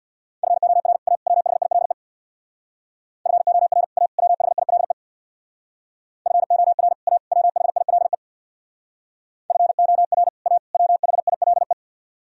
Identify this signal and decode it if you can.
Morse code, heard as a keyed tone, about 65 wpm, 700 Hz. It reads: FOR A WHILE FOR A WHILE FOR A WHILE FOR A WHILE